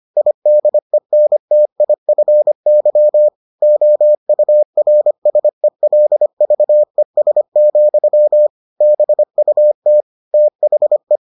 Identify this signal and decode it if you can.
Morse code, heard as a keyed tone, about 25 wpm, 600 Hz. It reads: IDENTIFY OURSELVES, BUT THE